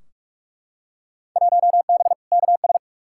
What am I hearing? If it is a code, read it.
Morse code, heard as a keyed tone, about 43 wpm, 700 Hz. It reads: 1B KS
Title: Field Day Exchanges Encore - 45wpm